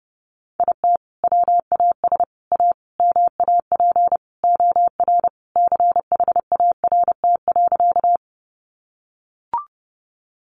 Morse code, heard as a keyed tone, about 30 wpm, 700 Hz. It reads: IT WAS A MAP OR CHART.
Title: Sentences from Top 1300 Words 30wpm